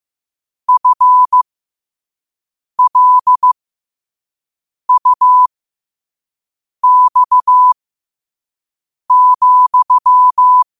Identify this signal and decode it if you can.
Morse code, heard as a keyed tone, about 15 wpm, spaced as for 5 wpm, 1000 Hz. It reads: FLUX,
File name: qotdfiles / 5wpm